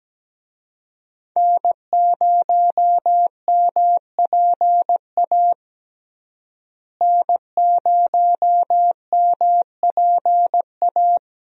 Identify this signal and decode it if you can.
Morse code, heard as a keyed tone, about 17 wpm, 700 Hz. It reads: N0MPA N0MPA
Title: USA Call Signs - 5 Characters - Repeated 2x - 17wpm